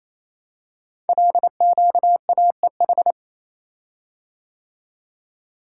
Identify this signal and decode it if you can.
Morse code, heard as a keyed tone, about 27 wpm, 700 Hz. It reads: LQAEH
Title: Code Groups - 5 Letters - 28wpm